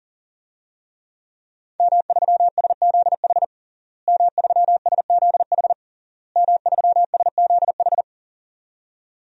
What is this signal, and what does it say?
Morse code, heard as a keyed tone, about 40 wpm, 700 Hz. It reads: M3SZH M3SZH M3SZH
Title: DXCC-100 Call Signs - 5 Characters - Repeated 3x - 40wpm